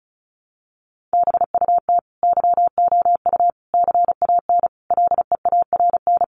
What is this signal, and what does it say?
Morse code, heard as a keyed tone, about 34 wpm, 700 Hz. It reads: BUT YOU CAN LEARN
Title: Sentences from Top 1500 Words 35wpm